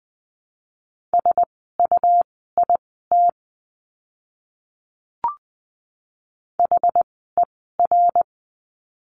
Morse code, heard as a keyed tone, about 20 wpm, spaced as for 15 wpm, 700 Hz. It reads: SUIT HER
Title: 101 Lesson 17f - Letter u - Words 20wpm